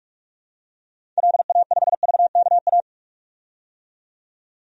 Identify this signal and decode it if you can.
Morse code, heard as a keyed tone, about 46 wpm, 700 Hz. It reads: LA5VKA
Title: DXCC-100 Call Signs - 6 Characters - 45wpm